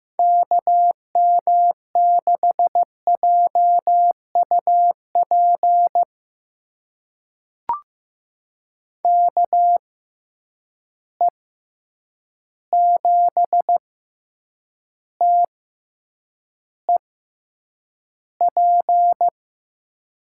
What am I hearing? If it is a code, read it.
Morse code, heard as a keyed tone, about 16 wpm, spaced as for 8 wpm, 700 Hz. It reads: KM6JUP KE7TEP
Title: USA Call Signs - 6 Characters - 6x Character Spacing - 15wpm